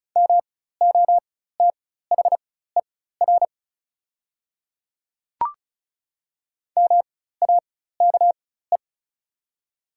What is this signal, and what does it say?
Morse code, heard as a keyed tone, about 36 wpm, spaced as for 16 wpm, 700 Hz. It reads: MOTHER MAKE
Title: 4x Character Spacing - Top 200 Words - 35wpm